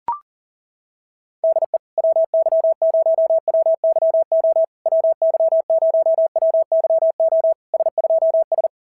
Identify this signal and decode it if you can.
Morse code, heard as a keyed tone, about 42 wpm, 650 Hz. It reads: DE WY0WYO WY0WYO S2S
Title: SOTA QSO Practice 40wpm